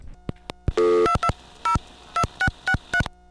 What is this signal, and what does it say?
DTMF, heard as telephone touch tones, tones of about 80 ms. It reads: 2202663